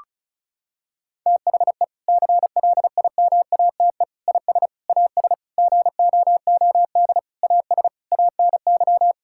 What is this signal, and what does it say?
Morse code, heard as a keyed tone, about 34 wpm, 700 Hz. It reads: THE CLIMATE IS AS GOOD AS ANY
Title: Sentences from Top 1600 Words 35wpm